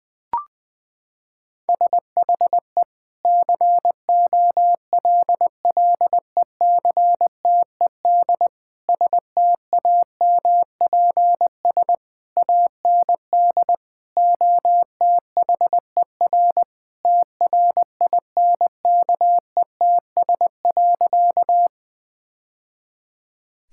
Morse code, heard as a keyed tone, about 21 wpm, 700 Hz. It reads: SHE COLLECTED STAMPS AND OTHER TRINKETS.
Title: Sentences with 2-Word Phrases - 20wpm